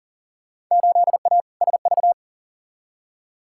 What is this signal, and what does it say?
Morse code, heard as a keyed tone, about 38 wpm, 700 Hz. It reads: 8A SV
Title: Field Day Exchanges Encore - 40wpm